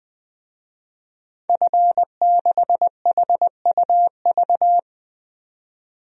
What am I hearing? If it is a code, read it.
Morse code, heard as a keyed tone, about 21 wpm, 700 Hz. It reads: F6HUV
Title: DXCC-100 Call Signs - 5 Characters - 20wpm